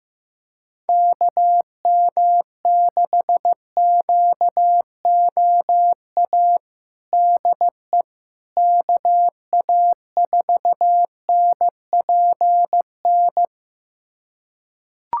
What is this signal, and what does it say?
Morse code, heard as a keyed tone, about 15 wpm, 700 Hz. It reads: KM6QOA DE KA4NPN
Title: USA Callsign DE Callsign - 6 Characters - 15wpm